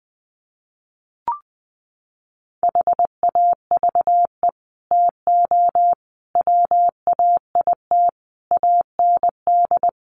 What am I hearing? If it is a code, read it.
Morse code, heard as a keyed tone, about 20 wpm, 700 Hz. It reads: HAVE TO WAIT AND
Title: Sets of 5 Words Encore2 20wpm